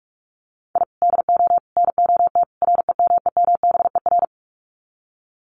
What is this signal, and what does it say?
Morse code, heard as a keyed tone, about 44 wpm, 700 Hz. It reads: I DO NOT REMEMBER
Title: Sets of 4 Words 45wpm